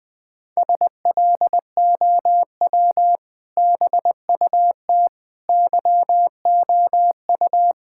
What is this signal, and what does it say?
Morse code, heard as a keyed tone, about 20 wpm, 700 Hz. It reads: SLOW BUT YOU
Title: Sentences with 2-Word Phrases - Spoken Beginning - 20wpm